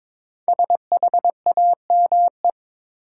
Morse code, heard as a keyed tone, about 22 wpm, 700 Hz. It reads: SHAME
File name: 5 Letter Words - First 3 Letters Given - 22wpm